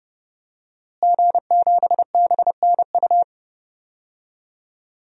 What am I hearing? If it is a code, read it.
Morse code, heard as a keyed tone, about 30 wpm, 700 Hz. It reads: G7BNU
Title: DXCC-100 Call Signs - 5 Characters - Repeated 3x - 30wpm